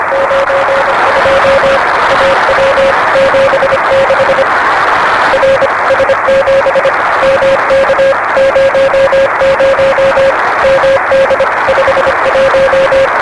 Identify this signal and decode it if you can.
Morse code, heard as a keyed tone, about 25 wpm, 560 Hz, with noise. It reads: 92 AW 76 R S7 MK 00 MD 51
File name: RS-7 Beacon 16 Feb 1985 oz6bl